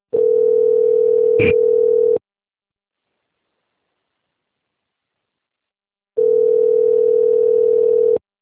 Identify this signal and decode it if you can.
Ringback tone (North America), 440 + 480 Hz.
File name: out- - - - - .1068